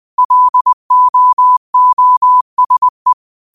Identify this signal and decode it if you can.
Morse code, heard as a keyed tone, about 20 wpm, 1000 Hz. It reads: LOOSE
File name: qotdfiles / 20wpm